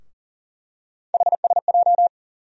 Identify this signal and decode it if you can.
Morse code, heard as a keyed tone, about 40 wpm, 700 Hz. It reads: HSJ
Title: USA Call Sign Postfixes - 3 Letters 40wpm